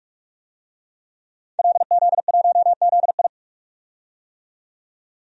Morse code, heard as a keyed tone, about 46 wpm, 700 Hz. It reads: LZ1ZI
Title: DXCC-100 Call Signs - 5 Characters - 45wpm